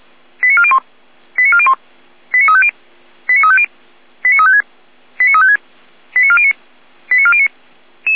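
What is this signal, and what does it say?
Five-tone selective call (ZVEI), heard as quick runs of five tones, tones of about 70 ms. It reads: 89401 89401 89350 89350 89356 89356 89409 89409
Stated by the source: ZVEI five-tone selective call 89164 received by SDR.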